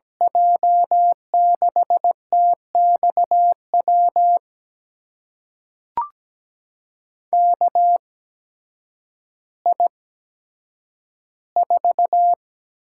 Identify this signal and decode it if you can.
Morse code, heard as a keyed tone, about 17 wpm, spaced as for 9 wpm, 700 Hz. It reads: J6TXW KI4
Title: USA Call Signs - 6 Characters - 8x Character Spacing - 17wpm